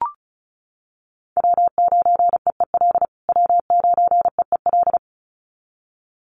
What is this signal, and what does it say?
Morse code, heard as a keyed tone, about 34 wpm, 700 Hz. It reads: W9EEL W9EEL
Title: USA Call Signs Repeated 2x 35wpm